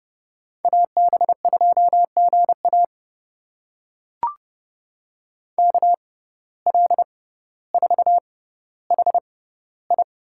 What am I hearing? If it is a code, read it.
Morse code, heard as a keyed tone, about 29 wpm, spaced as for 15 wpm, 700 Hz. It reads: AB2GA KL4HI
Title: USA Call Signs - 5 Characters - 6x Character Spacing - 30wpm